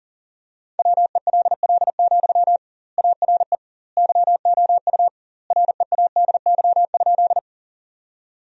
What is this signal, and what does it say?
Morse code, heard as a keyed tone, about 37 wpm, 700 Hz. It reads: WELL, ARE YOU READY?
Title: 2x Word-Spacing - Sentences from Top 400 Words - 40wpm